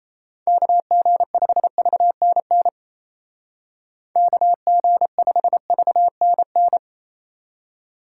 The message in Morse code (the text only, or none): KG5VNN KG5VNN